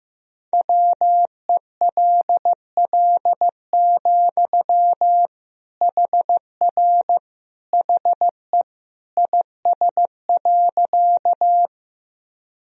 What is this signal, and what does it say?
Morse code, heard as a keyed tone, about 15 wpm, 700 Hz. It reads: WELL, HR HE IS.
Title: Sentences from Top 200 Words and Abbreviations 15wpm